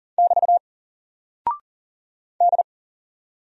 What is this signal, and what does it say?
Morse code, heard as a keyed tone, about 41 wpm, 700 Hz. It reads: <BT> D
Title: Single Character (Letter:Number:Punctuation:Prosign) 40wpm